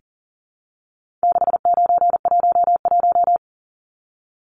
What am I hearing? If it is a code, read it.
Morse code, heard as a keyed tone, about 40 wpm, 700 Hz. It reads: 6911
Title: Numbers - 4-Digit 40wpm